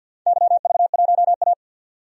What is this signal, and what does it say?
Morse code, heard as a keyed tone, about 51 wpm, 700 Hz. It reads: YV1A
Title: DXCC-100 Call Signs - 4 Characters - 50wpm